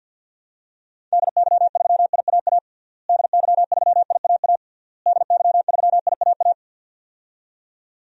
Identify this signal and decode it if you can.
Morse code, heard as a keyed tone, about 52 wpm, 700 Hz. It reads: DY3IAA DY3IAA DY3IAA
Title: DXCC-100 Call Signs - 6 Characters - Repeated 3x - 50wpm